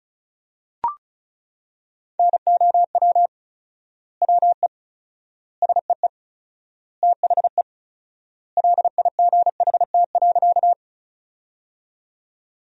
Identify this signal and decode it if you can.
Morse code, heard as a keyed tone, about 36 wpm, 700 Hz. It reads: NOW WE SEE THE LIGHT.
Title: 4x Word-Spacing - Sentences from Top 200 Words - 35wpm